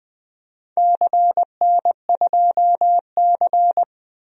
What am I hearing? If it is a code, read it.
Morse code, heard as a keyed tone, about 21 wpm, 700 Hz. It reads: CN2C
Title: DXCC-100 Call Signs - 4 Characters - 20wpm